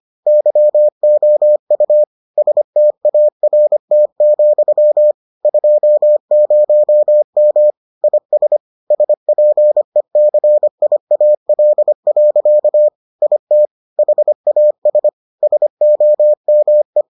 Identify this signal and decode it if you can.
Morse code, heard as a keyed tone, about 25 wpm, 600 Hz. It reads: YOU START, 20M IS SPECIAL. IT HAS SOME